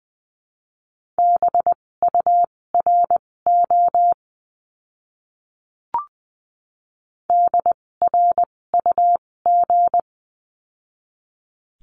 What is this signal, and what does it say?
Morse code, heard as a keyed tone, about 20 wpm, spaced as for 16 wpm, 700 Hz. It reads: BURO DRUG
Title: 256 Lesson 42a - Words 12-20wpm